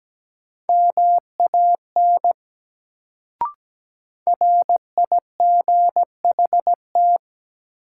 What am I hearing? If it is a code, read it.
Morse code, heard as a keyed tone, about 17 wpm, 700 Hz. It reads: MAN RIGHT
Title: Sets of 2 Words - New Word Drill - 17wpm